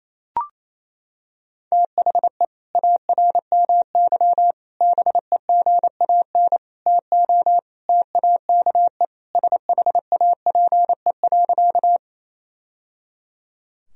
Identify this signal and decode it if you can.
Morse code, heard as a keyed tone, about 28 wpm, 700 Hz. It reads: THE ARMY BEGAN TO TAKE SHAPE.
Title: Sentences from Top 1000 Words - Review Part 3 - 28wpm